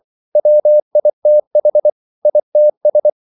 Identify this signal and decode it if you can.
Morse code, heard as a keyed tone, about 24 wpm, 600 Hz. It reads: WITH ITS